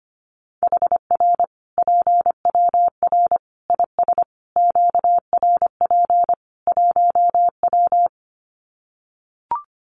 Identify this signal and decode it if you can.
Morse code, heard as a keyed tone, about 25 wpm, 700 Hz. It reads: HR PWR IS QRP 1W
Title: Common QSO Phrases 25wpm